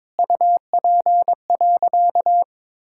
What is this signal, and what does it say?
Morse code, heard as a keyed tone, about 22 wpm, 700 Hz. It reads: UP.
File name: Sentences with 7-Word Phrases - 22wpm